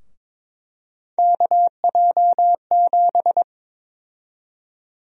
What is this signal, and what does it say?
Morse code, heard as a keyed tone, about 22 wpm, 700 Hz. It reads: KJ7
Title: USA Call Sign Prefixes - Repeated 2x - 22wpm